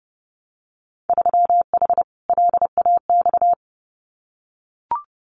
Morse code, heard as a keyed tone, about 32 wpm, 700 Hz. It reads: 3H LAX
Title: Winter Field Day Exchanges 30wpm